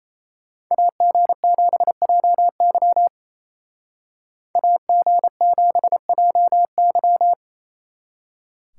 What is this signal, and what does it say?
Morse code, heard as a keyed tone, about 36 wpm, 700 Hz. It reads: AG7JY AG7JY